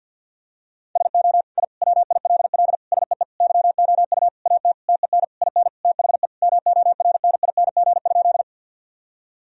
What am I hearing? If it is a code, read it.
Morse code, heard as a keyed tone, about 53 wpm, 700 Hz. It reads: SO I WILL SEE YOU AT TEN IN THE MORNING?
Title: Sentences from Top 1000 Words - Review Part 2 - 50wpm